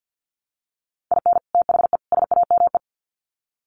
Morse code, heard as a keyed tone, about 50 wpm, 700 Hz. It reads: IN THE SAME
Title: Sets of 3 Words 50wpm